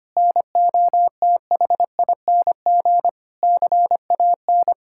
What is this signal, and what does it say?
Morse code, heard as a keyed tone, about 25 wpm, 700 Hz. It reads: NOTHING CAN
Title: Sentences from Top 1600 Words 25wpm